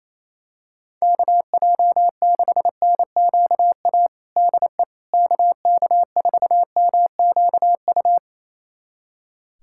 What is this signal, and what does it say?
Morse code, heard as a keyed tone, about 28 wpm, 700 Hz. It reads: KJ6NQA DE KK4MQU
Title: USA Callsign DE Callsign 28wpm